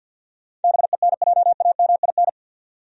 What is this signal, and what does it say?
Morse code, heard as a keyed tone, about 49 wpm, 700 Hz. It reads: BENJAMIN
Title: USA Common Names - 50wpm